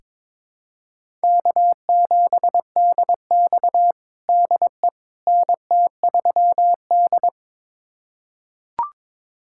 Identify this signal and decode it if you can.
Morse code, heard as a keyed tone, about 22 wpm, 700 Hz. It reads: K7DX DE NT3D
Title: USA Callsign DE Callsign - 4 Characters - 22wpm